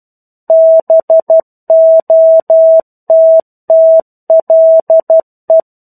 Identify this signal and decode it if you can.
Morse code, heard as a keyed tone, about 12 wpm, 650 Hz. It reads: BOTTLE
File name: 01-Chapter-01 12 WPM